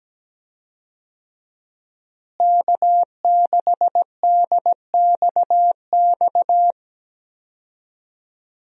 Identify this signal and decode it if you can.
Morse code, heard as a keyed tone, about 17 wpm, 700 Hz. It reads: K6DXX
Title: USA Call Signs - 5 Characters - Repeated 3x - 17wpm